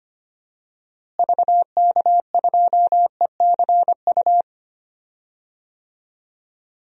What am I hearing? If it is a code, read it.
Morse code, heard as a keyed tone, about 25 wpm, 700 Hz. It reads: VK2ECU
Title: DXCC-100 Call Signs - 6 Characters - Repeated 3x - 25wpm